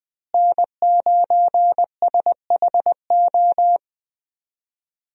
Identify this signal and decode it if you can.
Morse code, heard as a keyed tone, about 20 wpm, 700 Hz. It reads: N9SHO
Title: USA Call Signs - 5 Characters - Repeated 2x - 20wpm